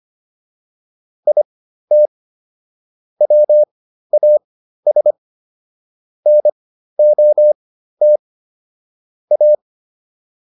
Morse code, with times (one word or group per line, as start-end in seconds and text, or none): 1.3-2.1 IT
3.2-5.1 WAS
6.3-8.2 NOT
9.3-9.5 A